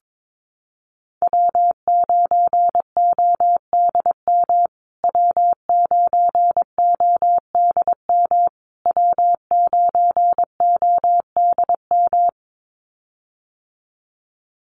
Morse code, heard as a keyed tone, about 22 wpm, 700 Hz. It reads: W9ODM W9ODM W9ODM
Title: USA Call Signs Repeated 3x 22wpm